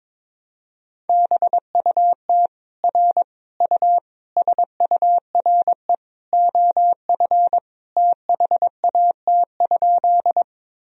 Morse code, heard as a keyed tone, about 22 wpm, 700 Hz. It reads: BUT R U SURE OF THAT?
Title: Sentences from Top 300 Words and Abbreviations 22wpm